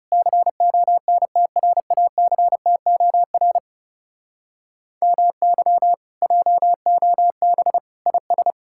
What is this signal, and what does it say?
Morse code, heard as a keyed tone, about 40 wpm, 700 Hz. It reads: CONTRACTOR MY JOB IS